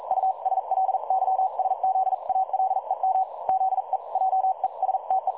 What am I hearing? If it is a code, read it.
Morse code, heard as a keyed tone, about 42 wpm, spaced as for 37 wpm, 760 Hz. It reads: USLYSZALEM ZE JESTE
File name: dlazaawansowanych / nagraniazpasma / websdr recording 3565.0kHz